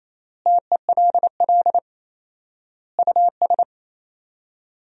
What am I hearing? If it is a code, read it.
Morse code, heard as a keyed tone, about 28 wpm, 700 Hz. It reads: TELL US
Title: 4x Word-Spacing - Sentences from Top 300 Words - 28wpm